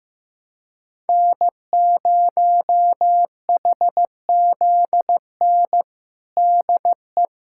Morse code, heard as a keyed tone, about 15 wpm, 700 Hz. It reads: N0HZN DE
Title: USA Callsign DE Callsign - 5 Characters - 15wpm